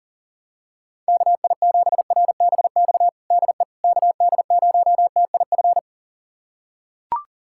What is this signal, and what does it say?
Morse code, heard as a keyed tone, about 40 wpm, 700 Hz. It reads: KI7RBX DE KD0TIF
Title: USA Callsign DE Callsign - 6 Characters - 40wpm